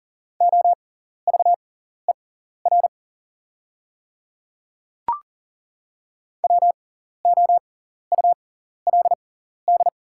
Morse code, heard as a keyed tone, about 40 wpm, spaced as for 14 wpm, 690 Hz. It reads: OVER WOULD
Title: 6x Character Spacing - Top 100 Words - 40wpm